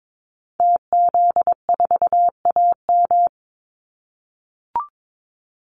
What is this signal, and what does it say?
Morse code, heard as a keyed tone, about 22 wpm, 700 Hz. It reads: TZ4AM
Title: Call Signs - 22wpm